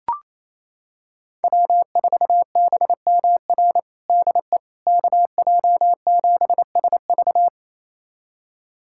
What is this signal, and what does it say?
Morse code, heard as a keyed tone, about 28 wpm, 700 Hz. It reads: W4BMR DE KJ7SV
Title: USA Callsign DE Callsign - 5 Characters - 28wpm